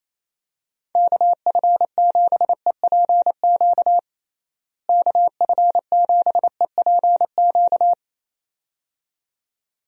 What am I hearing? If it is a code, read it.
Morse code, heard as a keyed tone, about 28 wpm, 700 Hz. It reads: KF7EPQ KF7EPQ
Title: USA Call Signs - 6 Characters - Repeated 2x - 28wpm